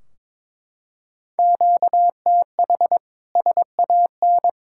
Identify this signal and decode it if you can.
Morse code, heard as a keyed tone, about 22 wpm, 700 Hz. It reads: QTH SAN
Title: Practice QSO - Exchange 1 - Beginning - 22wpm